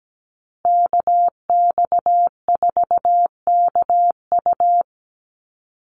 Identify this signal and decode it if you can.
Morse code, heard as a keyed tone, about 17 wpm, 700 Hz. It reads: KX4KU
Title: Call Signs Repeated 2x - 17wpm